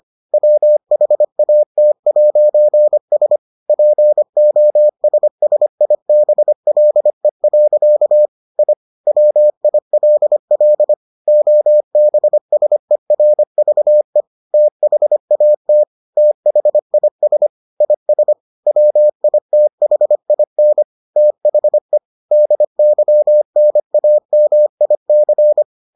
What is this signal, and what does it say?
Morse code, heard as a keyed tone, about 25 wpm, 600 Hz. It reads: WHAT'S POSSIBLE. I WILL OBSERVE THAT THIS IS WITHIN THE DYNAMIC